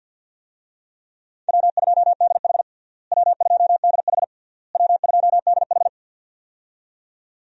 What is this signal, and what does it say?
Morse code, heard as a keyed tone, about 45 wpm, 700 Hz. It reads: W2DH W2DH W2DH
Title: USA Call Signs - 4 Characters - Repeated 3x - 50wpm